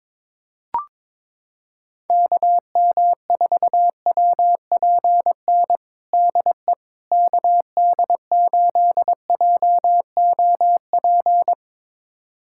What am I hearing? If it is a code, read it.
Morse code, heard as a keyed tone, about 21 wpm, 700 Hz. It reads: KM4WPN DE KD8JOP
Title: USA Callsign DE Callsign - 6 Characters - 22wpm